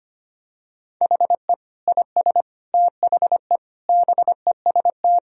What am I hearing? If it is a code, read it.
Morse code, heard as a keyed tone, about 25 wpm, 700 Hz. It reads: HE IS THE BEST
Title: Sentences from Top 1000 Words - Review Part 4 - 25wpm